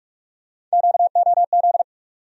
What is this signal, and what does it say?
Morse code, heard as a keyed tone, about 46 wpm, 700 Hz. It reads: QOZ
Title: USA Call Sign Postfixes - 3 Letters 45wpm